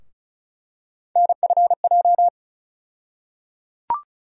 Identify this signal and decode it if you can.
Morse code, heard as a keyed tone, about 36 wpm, 700 Hz. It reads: NFJ
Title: USA Call Sign Postfixes - 3 Letters 35wpm